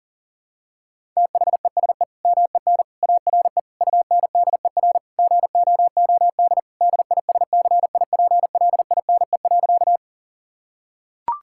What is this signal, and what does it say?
Morse code, heard as a keyed tone, about 40 wpm, 700 Hz. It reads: THESE MEN ARE UNDER GOOD DISCIPLINE.
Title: Sentences from Top 1000 Words - Review Part 6 - 40wpm